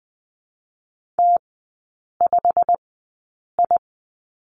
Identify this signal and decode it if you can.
Morse code, heard as a keyed tone, about 20 wpm, spaced as for 8 wpm, 700 Hz. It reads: T5I
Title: 127 Lesson 21e - Number 6 - Three Characters 20wpm